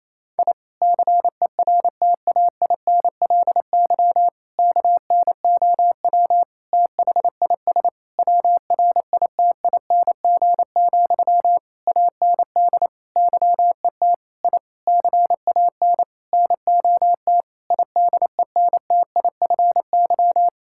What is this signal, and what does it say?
Morse code, heard as a keyed tone, about 28 wpm, 700 Hz. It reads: I CERTAINLY KNOW THIS WRITING, AND YET I CAN NOT IDENTIFY
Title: Sentences from Top 1000 Words - Review Part 6 - 28wpm